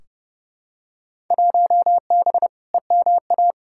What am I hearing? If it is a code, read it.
Morse code, heard as a keyed tone, about 30 wpm, 700 Hz. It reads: 1B EMA
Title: Field Day Exchanges - 30wpm